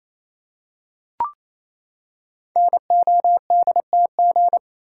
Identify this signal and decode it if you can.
Morse code, heard as a keyed tone, about 29 wpm, 700 Hz. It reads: NODTG
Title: Code Groups - 5 Letters - 28wpm